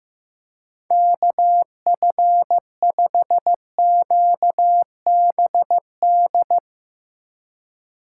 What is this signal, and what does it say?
Morse code, heard as a keyed tone, about 15 wpm, 700 Hz. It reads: KF5QBD
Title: USA Call Signs - 3x Character Spacing - 15wpm